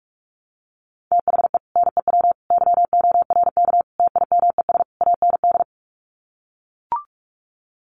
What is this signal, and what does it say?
Morse code, heard as a keyed tone, about 44 wpm, 700 Hz. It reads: THE NEW YORK TIMES AND
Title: Sets of 5 Words Encore 45wpm